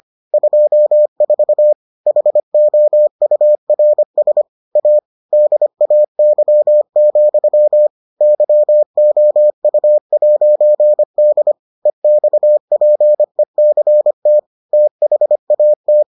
Morse code, heard as a keyed tone, about 25 wpm, 600 Hz. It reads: 24 HOURS A DAY, YOU'D EXPECT THAT